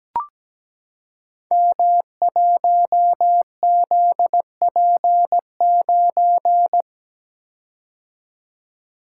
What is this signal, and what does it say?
Morse code, heard as a keyed tone, about 17 wpm, 700 Hz. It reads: M1ZP9
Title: Code Groups - 5 Letter:Numbers - 17wpm